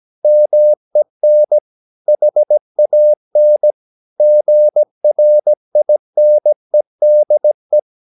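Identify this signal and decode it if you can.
Morse code, heard as a keyed tone, about 17 wpm, 600 Hz. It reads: MEN HAN GRINEDE
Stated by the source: Slow (QRS) Morse practice at 17 wpm, from CW med Gnister.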